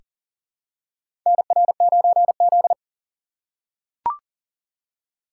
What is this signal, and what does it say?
Morse code, heard as a keyed tone, about 40 wpm, 700 Hz. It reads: NR9Z
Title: USA Call Signs - 4 Characters - 6x Character Spacing - 40wpm